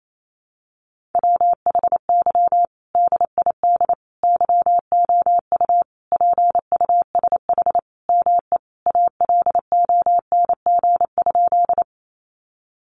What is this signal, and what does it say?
Morse code, heard as a keyed tone, about 28 wpm, 700 Hz. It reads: WHY DID YOU PUSH ME ALONG?
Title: Sentences from Top 1300 Words 28wpm